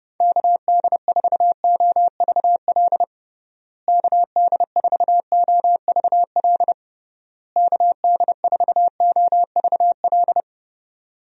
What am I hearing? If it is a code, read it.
Morse code, heard as a keyed tone, about 31 wpm, 700 Hz. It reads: KD4OVL KD4OVL KD4OVL
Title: USA Call Signs - 6 Characters - Repeated 3x - 30wpm